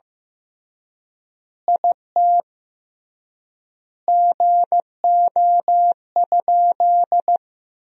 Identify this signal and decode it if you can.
Morse code, heard as a keyed tone, about 15 wpm, 700 Hz. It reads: IT GO?
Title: 3x Word-Spacing - Sentences from Top 200 Words - 15wpm